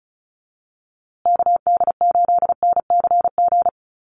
Morse code, heard as a keyed tone, about 34 wpm, 700 Hz. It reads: KD8NCG
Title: USA Call Signs Repeated 3x 35wpm